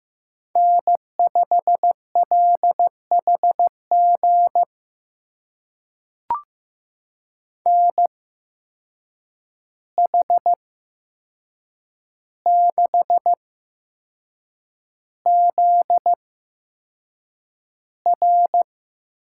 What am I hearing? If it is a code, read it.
Morse code, heard as a keyed tone, about 15 wpm, spaced as for 6 wpm, 700 Hz. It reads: N5LHG NH6ZR
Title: USA Call Signs - 5 Characters - 8x Character Spacing - 15wpm